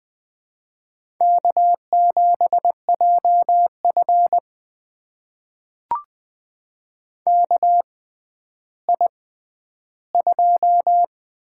Morse code, heard as a keyed tone, about 20 wpm, spaced as for 11 wpm, 700 Hz. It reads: K7JF KI2
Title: USA Call Signs - 4 Characters - 6x Character Spacing - 20wpm